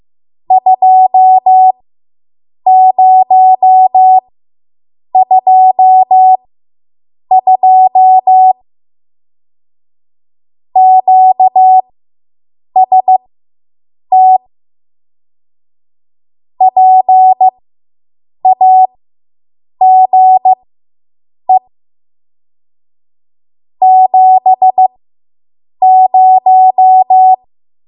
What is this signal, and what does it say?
Morse code, heard as a keyed tone, about 15 wpm, spaced as for 7 wpm, 750 Hz. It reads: 2022 QST PAGE 70